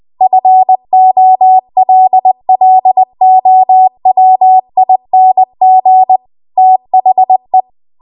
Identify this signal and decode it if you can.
Morse code, heard as a keyed tone, about 20 wpm, 750 Hz. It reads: FOLLOWING THE
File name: Archive / 20 WPM / 20WPM